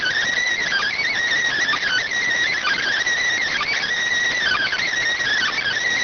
Emergency Alert System (SAME) tones (header bursts), heard as harsh buzzing bursts. It is heard from 0.3 to 4.9 s.